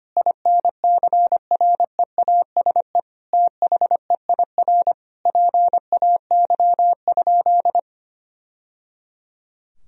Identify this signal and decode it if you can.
Morse code, heard as a keyed tone, about 25 wpm, 700 Hz. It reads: INCREASE THEIR PAY?
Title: Sentences from Top 1000 Words - Review Part 4 - 25wpm